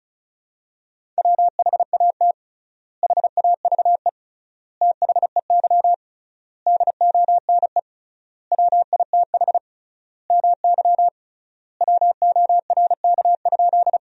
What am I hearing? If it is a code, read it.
Morse code, heard as a keyed tone, about 34 wpm, 700 Hz. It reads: WHAT HAVE THEY DONE WITH MY WORK?
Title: 3x Word-Spacing - Sentences from Top 500 Words - 35wpm